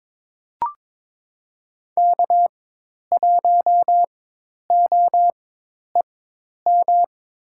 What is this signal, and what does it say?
Morse code, heard as a keyed tone, about 22 wpm, spaced as for 10 wpm, 700 Hz. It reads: K1OEM
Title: USA Call Signs - 4x Character Spacing - 22wpm